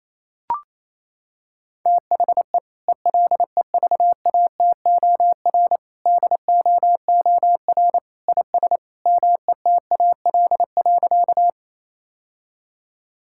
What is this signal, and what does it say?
Morse code, heard as a keyed tone, about 28 wpm, 700 Hz. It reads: THE ELEVATOR DOOR IS METAL.
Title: Sentences from Top 1600 Words 28wpm